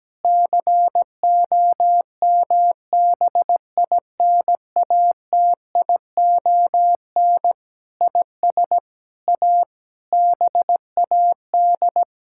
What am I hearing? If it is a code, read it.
Morse code, heard as a keyed tone, about 17 wpm, 700 Hz. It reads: COMBINATION IS A BAD
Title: Sentences from Top 1000 Words - Review Part 6 - 17wpm